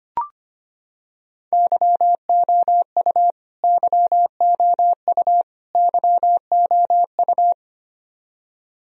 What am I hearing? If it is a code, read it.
Morse code, heard as a keyed tone, about 25 wpm, 700 Hz. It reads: YOU YOU YOU
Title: Top 100 Words Repeated 25wpm